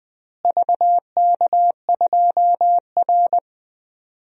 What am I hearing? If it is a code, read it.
Morse code, heard as a keyed tone, about 21 wpm, 700 Hz. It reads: VK2R
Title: DXCC-100 Call Signs - 4 Characters - Repeated 3x - 20wpm